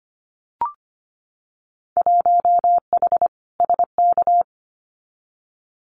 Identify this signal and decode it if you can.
Morse code, heard as a keyed tone, about 25 wpm, 700 Hz. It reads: 1H SK
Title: Winter Field Day Exchanges 25wpm